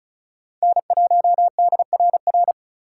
Morse code, heard as a keyed tone, about 36 wpm, 700 Hz. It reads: N1DRR
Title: USA Call Signs - 5 Characters - 2x Character Spacing - 35wpm